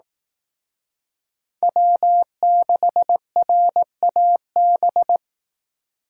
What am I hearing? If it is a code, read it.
Morse code, heard as a keyed tone, about 18 wpm, 700 Hz. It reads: W6RAB
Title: Speed Racing - USA Call Signs 15wpm